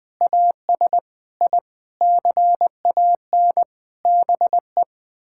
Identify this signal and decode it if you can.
Morse code, heard as a keyed tone, about 20 wpm, 700 Hz. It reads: AS I CAN BE
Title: Sets of 9 Words - 20wpm